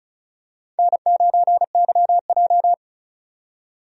Morse code, heard as a keyed tone, about 36 wpm, 700 Hz. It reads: N9YJ
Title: USA Call Signs - 4 Characters - 35wpm